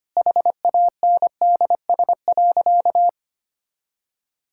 Morse code, heard as a keyed tone, about 25 wpm, 700 Hz. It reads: HANDS.
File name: practice / 25wpm / All-Age Appropriate Jokes - 25wpm